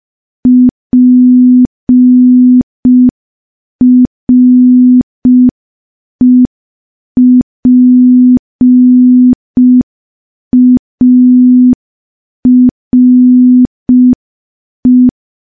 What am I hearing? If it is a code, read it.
Morse code, heard as a keyed tone, about 5 wpm, 250 Hz. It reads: PREPARE
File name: morse / morse-training-05WPM